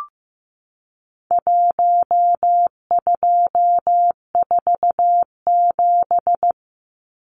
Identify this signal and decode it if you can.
Morse code, heard as a keyed tone, about 15 wpm, 700 Hz. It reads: 1247
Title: Numbers - 4-Digit 15wpm